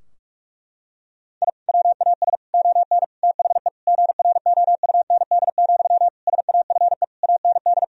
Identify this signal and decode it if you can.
Morse code, heard as a keyed tone, about 46 wpm, 700 Hz. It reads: I WAS ON THE GROUND, SAFE AND
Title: Sentences from Top 1000 Words - Review Part 2 - 45wpm